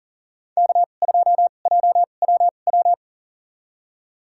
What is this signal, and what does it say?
Morse code, heard as a keyed tone, about 37 wpm, spaced as for 28 wpm, 700 Hz. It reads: K2JWW
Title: USA Call Signs - 5 Characters - 2x Character Spacing - 40wpm